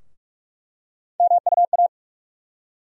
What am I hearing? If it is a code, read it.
Morse code, heard as a keyed tone, about 46 wpm, 700 Hz. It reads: MUA
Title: USA Call Sign Postfixes - 3 Letters 45wpm